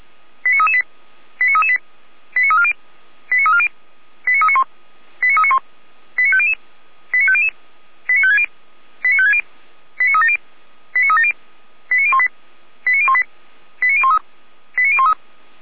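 Five-tone selective call (ZVEI), heard as quick runs of five tones, tones of about 70 ms. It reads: 89338 89338 89340 89340 89391 89391 89500 89500 89560 89560 89370 89370 89018 89018 89013 89013
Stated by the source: Five-tone ZVEI selective call received by SDR.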